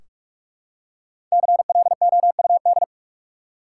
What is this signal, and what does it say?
Morse code, heard as a keyed tone, about 46 wpm, 700 Hz. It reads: CLOUD
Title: Five Letter Words - 45wpm